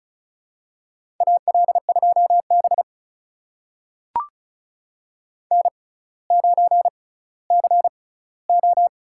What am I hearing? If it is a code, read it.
Morse code, heard as a keyed tone, about 36 wpm, spaced as for 18 wpm, 700 Hz. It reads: AL2B N9CO
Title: USA Call Signs - 4 Characters - 6x Character Spacing - 35wpm